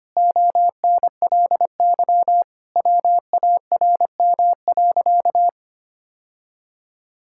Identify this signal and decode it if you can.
Morse code, heard as a keyed tone, about 25 wpm, 700 Hz. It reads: ONLY WARM.
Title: Sentences from Top 1000 Words - Review Part 6 - 25wpm